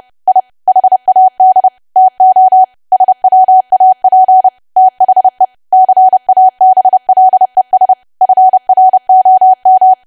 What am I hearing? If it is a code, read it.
Morse code, heard as a keyed tone, about 30 wpm, 750 Hz. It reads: I HAD TO SWAP THE CABLES FROM
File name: cwa / W1AW-Level3 / 30WPM